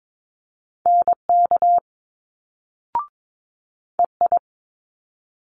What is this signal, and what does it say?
Morse code, heard as a keyed tone, about 22 wpm, 700 Hz. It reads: NK EI
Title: Multi-letter Phonemes Quick Fire 22wpm